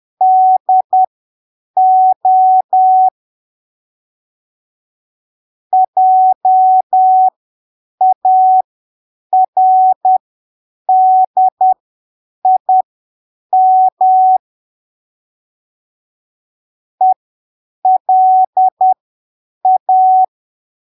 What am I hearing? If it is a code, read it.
Morse code, heard as a keyed tone, about 10 wpm, spaced as for 7 wpm, 750 Hz. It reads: DO JARDIM ELA
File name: arquivos / texto / texto 20 10ppm